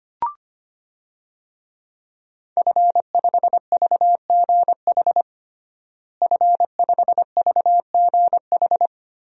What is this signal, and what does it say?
Morse code, heard as a keyed tone, about 25 wpm, 700 Hz. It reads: F5VGH F5VGH
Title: DXCC-100 Call Signs - 5 Characters - Repeated 3x - 25wpm